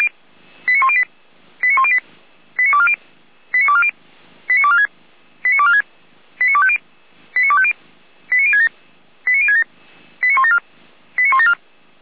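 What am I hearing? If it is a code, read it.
Five-tone selective call (ZVEI), heard as quick runs of five tones, tones of about 70 ms. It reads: 89108 89108 89240 89240 89246 89246 89250 89250 89067 89067 89174 89174
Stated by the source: ZVEI five-tone selective call received by SDR, from the Feuerwehr Orlamünde (fire brigade) audio archive.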